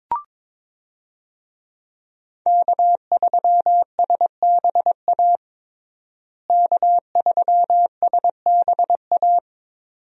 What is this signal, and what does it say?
Morse code, heard as a keyed tone, about 22 wpm, 700 Hz. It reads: K3SBA K3SBA
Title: USA Call Signs - 5 Characters - Repeated 2x - 22wpm